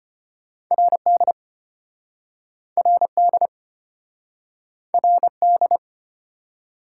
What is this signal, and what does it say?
Morse code, heard as a keyed tone, about 30 wpm, 700 Hz. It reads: RD RD RD
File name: practice / 25wpm / Speed-Racing - Two Letter-Number Combinations - 25wpm